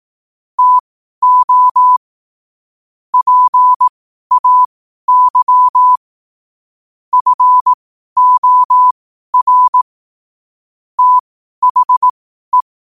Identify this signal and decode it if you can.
Morse code, heard as a keyed tone, about 18 wpm, spaced as for 12 wpm, 1000 Hz. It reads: TO PAY FOR THE
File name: qotdfiles / 12wpm